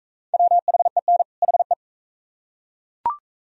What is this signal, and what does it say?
Morse code, heard as a keyed tone, about 40 wpm, 700 Hz. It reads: WHEN HE